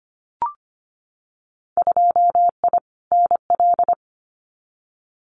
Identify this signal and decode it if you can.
Morse code, heard as a keyed tone, about 25 wpm, 700 Hz. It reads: 2I NL
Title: Winter Field Day Exchanges Encore 25wpm